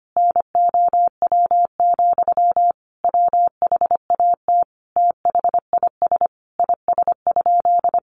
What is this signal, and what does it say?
Morse code, heard as a keyed tone, about 25 wpm, 700 Hz. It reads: NOW, WHAT THIS IS?
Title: Sentences from Top 1500 Words 25wpm